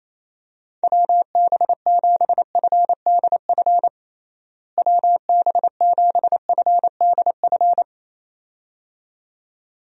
Morse code, heard as a keyed tone, about 28 wpm, 700 Hz. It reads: WB7FDF WB7FDF
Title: USA Call Signs - 6 Characters - Repeated 2x - 28wpm